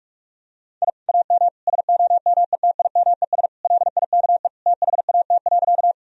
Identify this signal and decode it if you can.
Morse code, heard as a keyed tone, about 44 wpm, 700 Hz. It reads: I AM SOMETIMES LIKE THAT.
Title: Sentences from Top 1000 Words - Review Part 3 - 45wpm